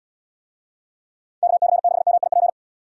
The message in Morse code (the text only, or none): KC6MIP